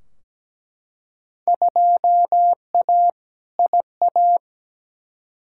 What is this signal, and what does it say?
Morse code, heard as a keyed tone, about 17 wpm, 700 Hz. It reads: 2A IA
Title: Field Day Exchanges - 17wpm